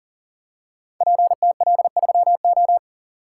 Morse code, heard as a keyed tone, about 40 wpm, 700 Hz. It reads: PTL3O
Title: Code Groups - 5 Letter:Numbers - 40wpm